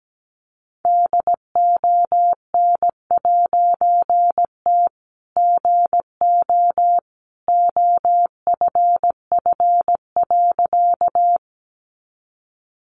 Morse code, heard as a keyed tone, about 17 wpm, 700 Hz. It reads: DON'T GO OFF.
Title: A Sound Of Thunder 17wpm